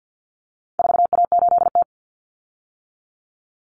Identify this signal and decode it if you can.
Morse code, heard as a keyed tone, about 50 wpm, 700 Hz. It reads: 4A8T
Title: Common Cut Numbers - 4-Digit 50wpm